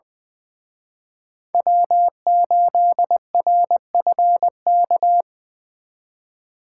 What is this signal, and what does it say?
Morse code, heard as a keyed tone, about 24 wpm, 700 Hz. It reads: W8RFK
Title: Speed Racing - USA 1x3 Call Signs 20wpm